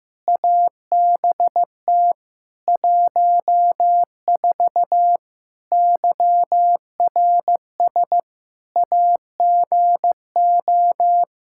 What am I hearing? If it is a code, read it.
Morse code, heard as a keyed tone, about 15 wpm, 700 Hz. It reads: ABT 14 YRS AGO
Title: Practice QSO - Exchange 3 - More Info - 15wpm